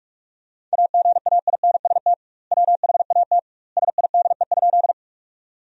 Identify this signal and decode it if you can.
Morse code, heard as a keyed tone, about 43 wpm, 700 Hz. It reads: AGAINST WHAT SIDE?
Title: 2x Word-Spacing - Sentences from Top 400 Words - 45wpm